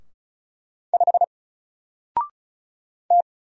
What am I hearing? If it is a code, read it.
Morse code, heard as a keyed tone, about 36 wpm, 700 Hz. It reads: 5 T
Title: Single Letter-Number 35wpm